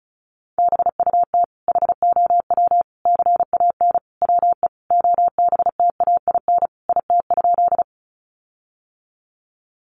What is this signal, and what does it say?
Morse code, heard as a keyed tone, about 34 wpm, 700 Hz. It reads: BUT HOW CAN WE OBTAIN IT?
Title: Sentences from Top 1300 Words 35wpm